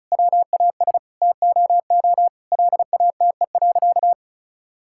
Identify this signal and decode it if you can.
Morse code, heard as a keyed tone, about 34 wpm, 700 Hz. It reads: WAS TOO LATE.
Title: Sentences from Top 1000 Words - Review Part 6 - 35wpm